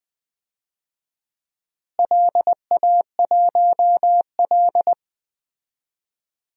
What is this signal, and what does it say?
Morse code, heard as a keyed tone, about 20 wpm, 700 Hz. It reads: LA1L
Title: DXCC-100 Call Signs - 4 Characters - Repeated 3x - 20wpm